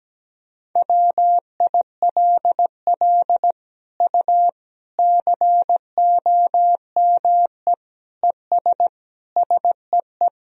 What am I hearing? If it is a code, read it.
Morse code, heard as a keyed tone, about 17 wpm, 700 Hz. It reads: WILL U COME ES SEE
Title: Sentences from Top 100 Words and Abbreviations 17wpm